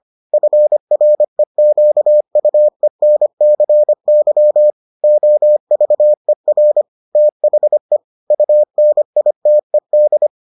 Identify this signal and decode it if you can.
Morse code, heard as a keyed tone, about 25 wpm, 600 Hz. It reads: FREQUENCY OVER THE UNITED